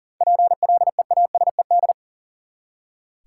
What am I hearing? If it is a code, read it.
Morse code, heard as a keyed tone, about 40 wpm, 700 Hz. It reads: PLEASED
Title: Seven Letter Words - 40wpm